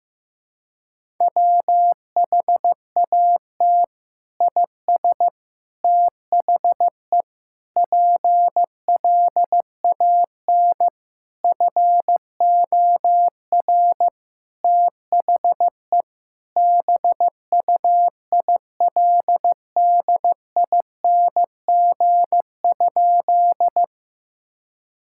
Morse code, heard as a keyed tone, about 15 wpm, 700 Hz. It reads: WHAT IS THE PLAN FOR THE BUILDING?
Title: Sentences from Top 1000 Words - Review Part 5 - 15wpm